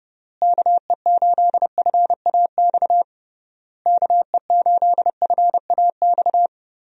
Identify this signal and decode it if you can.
Morse code, heard as a keyed tone, about 30 wpm, 700 Hz. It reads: KE8FAX KE8FAX
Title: USA Call Signs - 6 Characters - Repeated 3x - 30wpm